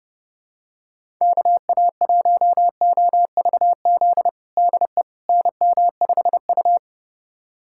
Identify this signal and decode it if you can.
Morse code, heard as a keyed tone, about 32 wpm, 700 Hz. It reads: KA1OVZ DE NM5U
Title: USA Callsign DE Callsign 30wpm